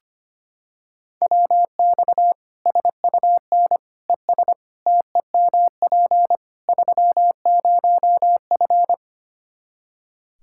Morse code, heard as a keyed tone, about 25 wpm, 700 Hz. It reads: WX SUN ES TEMP 30F
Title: Practice QSO - Exchange 2 - Info - 25wpm